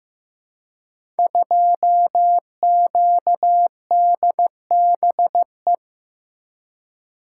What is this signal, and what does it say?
Morse code, heard as a keyed tone, about 15 wpm, 700 Hz. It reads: 2QDBE
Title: Code Groups - 5 Letter:Numbers - 15wpm